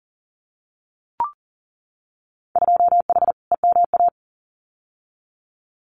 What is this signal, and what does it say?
Morse code, heard as a keyed tone, about 40 wpm, 700 Hz. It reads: 2H EMA
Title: Winter Field Day Exchanges 40wpm